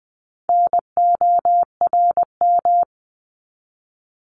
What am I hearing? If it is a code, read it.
Morse code, heard as a keyed tone, about 20 wpm, 700 Hz. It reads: NORM
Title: 260 Lesson 42e - Words 20-20wpm